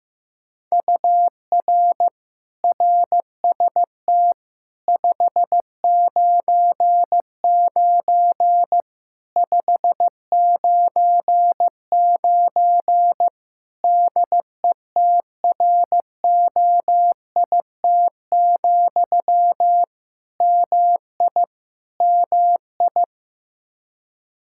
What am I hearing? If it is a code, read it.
Morse code, heard as a keyed tone, about 15 wpm, 700 Hz. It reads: UR RST 599 599 DETROIT, MI MI
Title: Practice QSO - Exchange 1 - Beginning - 15wpm